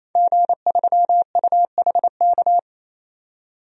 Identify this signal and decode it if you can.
Morse code, heard as a keyed tone, about 28 wpm, 700 Hz. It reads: G3UHK
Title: DXCC-100 Call Signs - 5 Characters - 28wpm